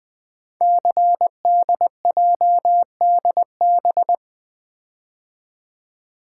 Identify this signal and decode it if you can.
Morse code, heard as a keyed tone, about 20 wpm, 700 Hz. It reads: CDJDB
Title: Code Groups - 5 Letters - 20wpm